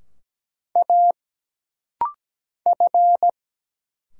Morse code, heard as a keyed tone, about 17 wpm, 700 Hz. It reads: A F
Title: Single Letter-Number Alt - Rapid-Fire 17wpm